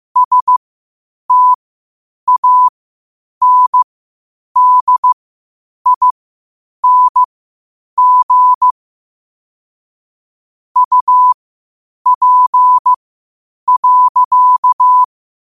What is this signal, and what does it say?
Morse code, heard as a keyed tone, about 15 wpm, spaced as for 8 wpm, 1000 Hz. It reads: STANDING UP.
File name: qotdfiles / 8wpm